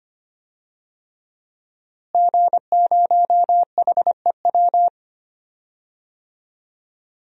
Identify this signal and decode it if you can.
Morse code, heard as a keyed tone, about 25 wpm, 700 Hz. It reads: G0HEW
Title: DXCC-100 Call Signs - 5 Characters - Repeated 3x - 25wpm